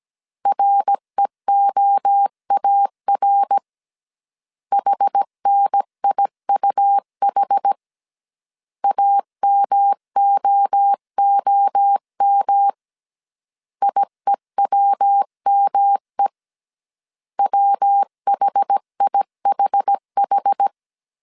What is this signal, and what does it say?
Morse code, heard as a keyed tone, about 17 wpm, 800 Hz. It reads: REOAR HNIUH AMOOM IEWME WHIHH
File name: CW / DET / 1 3 1500C S A 16wpm 2141